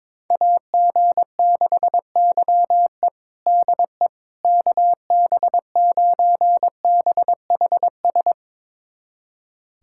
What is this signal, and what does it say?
Morse code, heard as a keyed tone, about 22 wpm, 700 Hz. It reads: AG6YE DE KB9BHS
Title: USA Callsign DE Callsign 22wpm